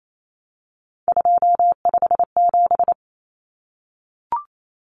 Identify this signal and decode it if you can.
Morse code, heard as a keyed tone, about 28 wpm, 700 Hz. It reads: 257
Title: Common Cut Numbers - 3-Digit 28wpm